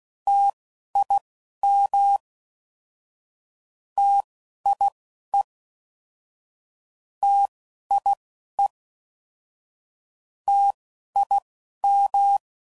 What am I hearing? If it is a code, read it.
Morse code, heard as a keyed tone, about 16 wpm, spaced as for 11 wpm, 800 Hz. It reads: TIM TIE TIE TIM